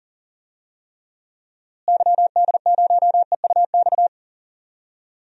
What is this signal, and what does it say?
Morse code, heard as a keyed tone, about 40 wpm, 700 Hz. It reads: YD0EUX
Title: DXCC-100 Call Signs - 6 Characters - Repeated 3x - 40wpm